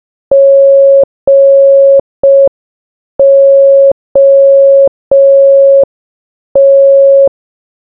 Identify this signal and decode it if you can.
Morse code, heard as a keyed tone, about 5 wpm, 550 Hz. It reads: GOT
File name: baba-is-you-the-folder-megapack-linux-macos-windows-gamerip-2024 / dobyxyjbym / 30. Morse